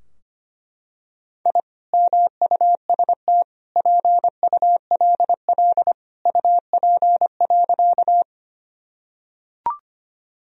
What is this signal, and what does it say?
Morse code, heard as a keyed tone, about 25 wpm, 700 Hz. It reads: I MUST PULL UP.
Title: Sentences from Top 1000 Words - Review Part 2 - 25wpm